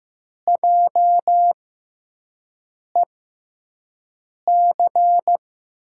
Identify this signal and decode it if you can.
Morse code, heard as a keyed tone, about 15 wpm, spaced as for 5 wpm, 700 Hz. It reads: JEC
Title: USA Call Signs - 5 Characters - 6x Character Spacing - 15wpm